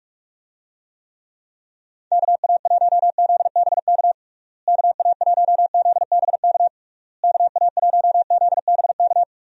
Morse code, heard as a keyed tone, about 47 wpm, 700 Hz. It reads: KA1ZBK KA1ZBK KA1ZBK
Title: USA Call Signs - 6 Characters - Repeated 3x - 45wpm